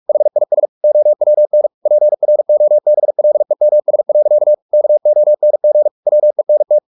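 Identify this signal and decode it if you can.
Morse code, heard as a keyed tone, about 44 wpm, 600 Hz. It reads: HIS OWN PROBLEMS. KONG WENT
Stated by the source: Morse practice at 45 wpm, the novel Never by Ken Follett in English, from CW med Gnister.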